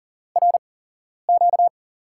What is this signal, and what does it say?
Morse code, heard as a keyed tone, about 40 wpm, spaced as for 11 wpm, 700 Hz. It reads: RQ
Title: USA Call Signs - 6 Characters - 8x Character Spacing - 40wpm